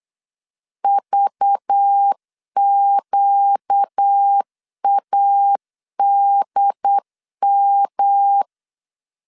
Morse code, heard as a keyed tone, about 8 wpm, 800 Hz. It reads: VQADM
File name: CW / DET / 1 6 1500C S B 8wpm 5142